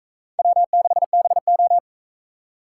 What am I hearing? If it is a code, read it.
Morse code, heard as a keyed tone, about 40 wpm, 700 Hz. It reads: W6BO